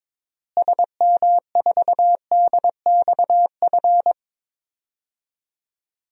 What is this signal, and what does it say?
Morse code, heard as a keyed tone, about 22 wpm, 700 Hz. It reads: SM4DXF
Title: DXCC-100 Call Signs - 6 Characters - Repeated 2x - 22wpm